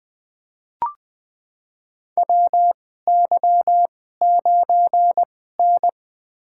Morse code, heard as a keyed tone, about 20 wpm, spaced as for 15 wpm, 700 Hz. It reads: WY9N
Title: USA Call Signs - 4 Characters - 2x Character Spacing - 20wpm